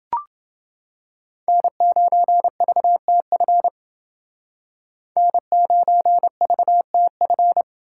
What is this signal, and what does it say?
Morse code, heard as a keyed tone, about 30 wpm, 700 Hz. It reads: N9VTF N9VTF